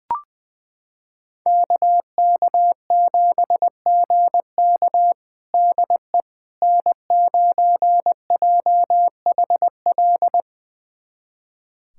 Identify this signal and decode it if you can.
Morse code, heard as a keyed tone, about 20 wpm, 700 Hz. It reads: KK7GK DE N9JHL
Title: USA Callsign DE Callsign - 5 Characters - 20wpm